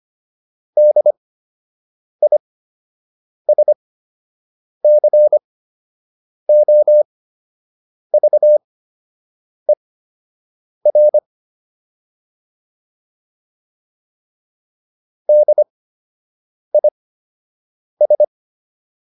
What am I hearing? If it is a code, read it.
Morse code, heard as a keyed tone, about 25 wpm, spaced as for 7 wpm, 600 Hz. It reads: DISCOVER DIS